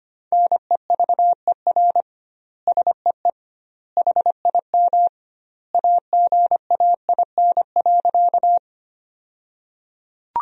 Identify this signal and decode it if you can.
Morse code, heard as a keyed tone, about 25 wpm, 700 Hz. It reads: NEVER SEE HIM AGAIN.
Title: 2x Word-Spacing - Sentences from Top 300 Words - 25wpm